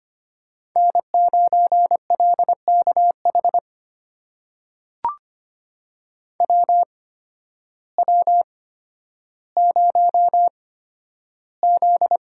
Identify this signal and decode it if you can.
Morse code, heard as a keyed tone, about 25 wpm, spaced as for 12 wpm, 700 Hz. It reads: N9LKH WW0Z
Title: USA Call Signs - 8x Character Spacing - 25wpm